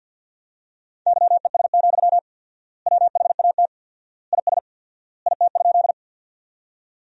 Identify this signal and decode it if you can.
Morse code, heard as a keyed tone, about 53 wpm, 700 Hz. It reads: YES, WHAT IS IT?
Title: 4x Word-Spacing - Sentences from Top 500 Words - 50wpm